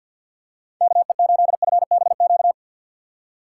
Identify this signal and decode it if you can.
Morse code, heard as a keyed tone, about 47 wpm, 700 Hz. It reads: KE8FBQ
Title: USA Call Signs - 6 Characters - 3x Character Spacing - 50wpm